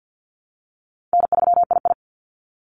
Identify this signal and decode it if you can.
Morse code, heard as a keyed tone, about 50 wpm, 700 Hz. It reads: N3II
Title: NA SOTA Activator Callsigns 50wpm